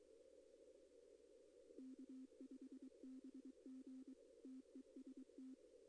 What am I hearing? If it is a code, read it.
Morse code, heard as a keyed tone, about 23 wpm, 260 Hz. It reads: K5BG TEST
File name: contests / arrl-160-k5tr-op-w5tn -2200